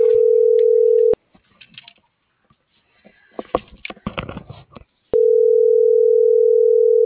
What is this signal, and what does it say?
Ringback tone (North America), 440 + 480 Hz.